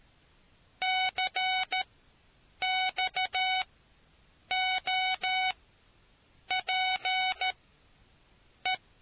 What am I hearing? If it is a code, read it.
Morse code, heard as a keyed tone, about 13 wpm, spaced as for 7 wpm, 750 Hz. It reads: CXOPE